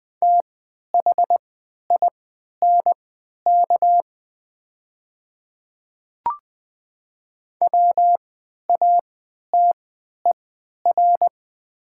Morse code, heard as a keyed tone, about 20 wpm, spaced as for 11 wpm, 700 Hz. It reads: THINK WATER
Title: 3x Character Spacing - Top 200 Words - 20wpm